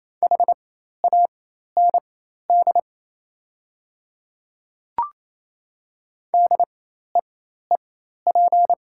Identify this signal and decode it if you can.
Morse code, heard as a keyed tone, about 28 wpm, spaced as for 13 wpm, 700 Hz. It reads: HAND DEEP
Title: 4x Character Spacing - Top 500 Words - 28wpm